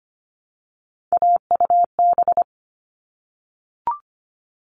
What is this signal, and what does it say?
Morse code, heard as a keyed tone, about 25 wpm, 700 Hz. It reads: AUB
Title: All Cut Numbers - 3-Digit 25wpm